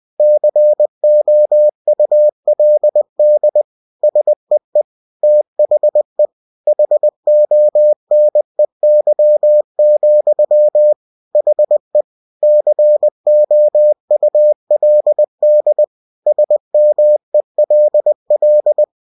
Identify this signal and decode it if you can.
Morse code, heard as a keyed tone, about 20 wpm, 600 Hz. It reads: COULD SEE THE HONEY, HE COULD SMELL
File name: cw-books / winnie-the-pooh / winnie-20wpm-page-0007